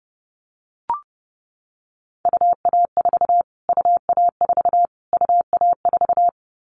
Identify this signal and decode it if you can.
Morse code, heard as a keyed tone, about 30 wpm, 700 Hz. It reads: UA4 UA4 UA4
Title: Callsign Prefixes for Top 25 DXCC - Repeated 3x - 30wpm